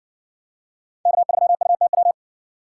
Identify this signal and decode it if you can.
Morse code, heard as a keyed tone, about 64 wpm, 700 Hz. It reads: K3UTW